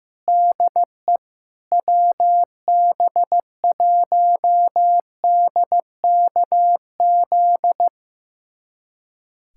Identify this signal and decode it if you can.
Morse code, heard as a keyed tone, about 16 wpm, 700 Hz. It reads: DE WB1DKZ
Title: USA Callsign DE Callsign 15wpm